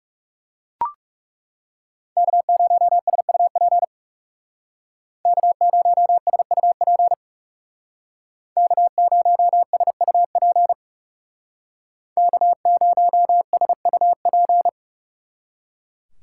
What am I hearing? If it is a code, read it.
Morse code, heard as a keyed tone, about 36 wpm, 700 Hz. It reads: K0SUP K0SUP K0SUP K0SUP
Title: Speed Racing - USA 1x3 Call Signs 30wpm